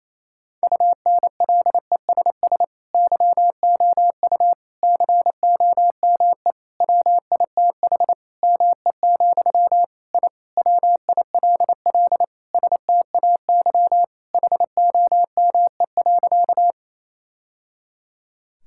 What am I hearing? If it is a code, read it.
Morse code, heard as a keyed tone, about 28 wpm, 700 Hz. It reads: UNLESS YOU COME WITH ME, I WILL STAY HOME.
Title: Sentences from Top 1000 Words - Review Part 6 - 28wpm